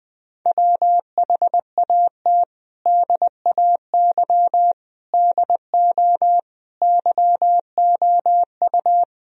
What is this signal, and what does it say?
Morse code, heard as a keyed tone, about 20 wpm, 700 Hz. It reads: WHAT DAY DO YOU
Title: Sentences from Top 1000 Words - Review Part 3 - 20wpm